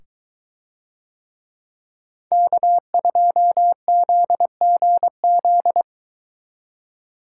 Morse code, heard as a keyed tone, about 22 wpm, 700 Hz. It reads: K2ZGZ